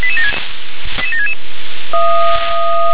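Five-tone selective call (ZVEI), heard as quick runs of five tones, tones of about 70 ms. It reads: 88766
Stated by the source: ZVEI five-tone selective call, received by SDR.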